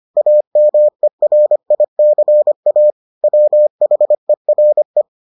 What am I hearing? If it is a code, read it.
Morse code, heard as a keyed tone, about 25 wpm, 600 Hz. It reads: AMERICA WHERE